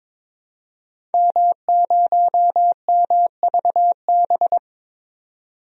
Morse code, heard as a keyed tone, about 21 wpm, 700 Hz. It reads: M0MVB
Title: DXCC-100 Call Signs - 5 Characters - 22wpm